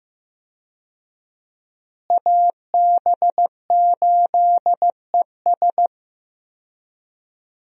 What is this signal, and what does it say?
Morse code, heard as a keyed tone, about 15 wpm, 700 Hz. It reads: AB8ES
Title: USA Call Signs - 5 Characters - Repeated 2x - 15wpm